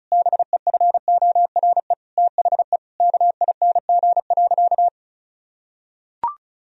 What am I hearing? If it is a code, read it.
Morse code, heard as a keyed tone, about 34 wpm, 700 Hz. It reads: BEFORE THE KING.
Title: Sentences from Top 1000 Words - Review Part 2 - 35wpm